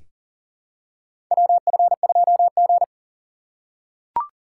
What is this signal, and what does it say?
Morse code, heard as a keyed tone, about 40 wpm, 700 Hz. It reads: WF2G
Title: USA Call Signs - 4 Characters - 3x Character Spacing - 40wpm